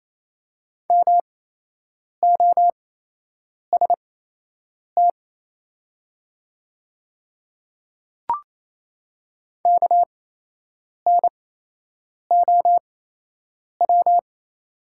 Morse code, heard as a keyed tone, about 28 wpm, spaced as for 8 wpm, 700 Hz. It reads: MOST KNOW
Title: 8x Character Spacing - Top 100 Words - 28wpm